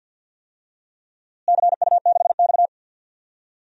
Morse code, heard as a keyed tone, about 53 wpm, 700 Hz. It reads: CU6X
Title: DXCC-100 Call Signs - 4 Characters - Repeated 2x - 50wpm